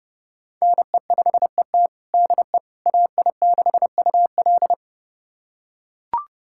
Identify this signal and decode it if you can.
Morse code, heard as a keyed tone, about 30 wpm, 700 Hz. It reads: NE5ET DE AI6UL
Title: USA Callsign DE Callsign - 5 Characters - 30wpm